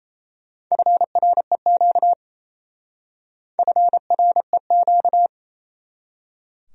Morse code, heard as a keyed tone, about 35 wpm, 700 Hz. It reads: FREQ FREQ
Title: Speed-Racing Abbreviations - 28wpm